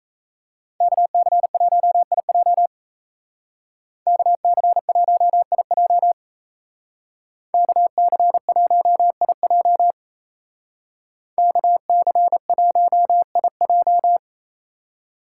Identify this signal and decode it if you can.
Morse code, heard as a keyed tone, about 33 wpm, 700 Hz. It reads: KC1IJ KC1IJ KC1IJ KC1IJ
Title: Speed Racing - USA 2x2 Call Signs 28wpm